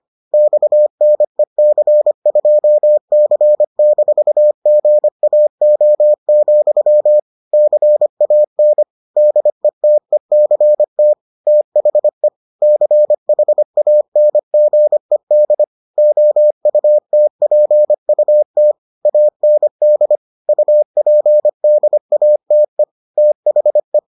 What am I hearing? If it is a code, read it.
Morse code, heard as a keyed tone, about 25 wpm, 600 Hz. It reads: XNEC2C-GAO, CAN DETECT THE CHANGED OUTPUT AND UPDATE THE